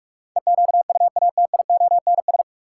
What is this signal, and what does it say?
Morse code, heard as a keyed tone, about 46 wpm, 700 Hz. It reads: EQUATIONS
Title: Sets of 5 Words - New Word Drill - 45wpm